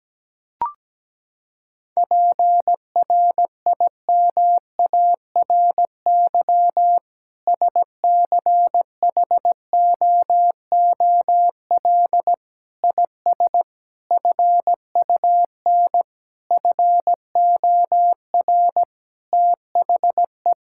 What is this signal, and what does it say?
Morse code, heard as a keyed tone, about 17 wpm, 700 Hz. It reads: PRIMARY SCHOOL IS FUN FOR THE
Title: Sentences from Top 1000 Words - Review Part 5 - 17wpm